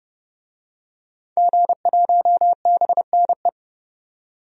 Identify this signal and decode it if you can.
Morse code, heard as a keyed tone, about 32 wpm, 700 Hz. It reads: G1BNE
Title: DXCC-100 Call Signs - 5 Characters - 30wpm